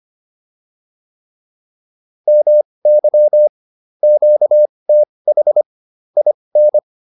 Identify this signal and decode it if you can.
Morse code, heard as a keyed tone, about 25 wpm, spaced as for 20 wpm, 600 Hz. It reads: MY QTH IN
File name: QSO205 20